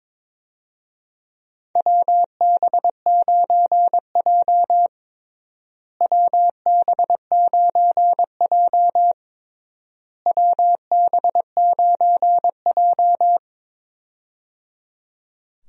Morse code, heard as a keyed tone, about 22 wpm, 700 Hz. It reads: WB9J WB9J WB9J
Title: USA Call Signs - 4 Characters - Repeated 3x - 22wpm